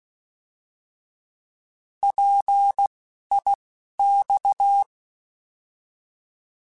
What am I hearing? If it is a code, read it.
Morse code, heard as a keyed tone, about 16 wpm, spaced as for 11 wpm, 800 Hz. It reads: PIX
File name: Alphabetical / Alphabetical mp3s / Ps @10